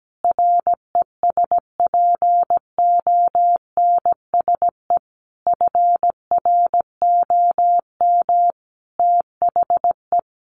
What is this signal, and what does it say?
Morse code, heard as a keyed tone, about 17 wpm, 700 Hz. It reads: RESPONSE FROM THE
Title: Sentences from Top 1400 Words 17wpm